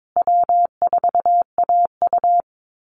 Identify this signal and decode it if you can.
Morse code, heard as a keyed tone, about 22 wpm, 700 Hz. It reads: W4AU
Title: Easy Call Signs Repeated 3x - 22wpm